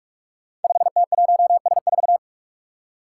Morse code, heard as a keyed tone, about 43 wpm, 700 Hz. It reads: 5T1S4
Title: Five Challenging Letter-Number Combinations 45wpm